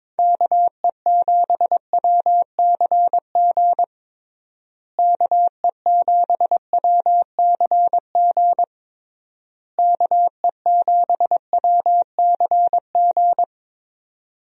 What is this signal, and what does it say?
Morse code, heard as a keyed tone, about 22 wpm, 700 Hz. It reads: KE7WCG KE7WCG KE7WCG
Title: USA Call Signs - 6 Characters - Repeated 3x - 22wpm